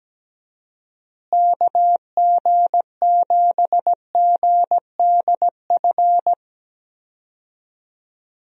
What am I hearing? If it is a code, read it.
Morse code, heard as a keyed tone, about 17 wpm, 700 Hz. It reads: KG7GDF
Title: USA Call Signs - 6 Characters - Repeated 3x - 17wpm